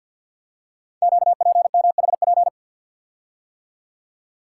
Morse code, heard as a keyed tone, about 51 wpm, 700 Hz. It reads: QPMHP
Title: Code Groups - 5 Letters - 50wpm